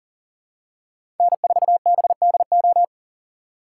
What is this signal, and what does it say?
Morse code, heard as a keyed tone, about 40 wpm, 700 Hz. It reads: N4BDO
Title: USA Call Signs - 5 Characters - 4x Character Spacing - 40wpm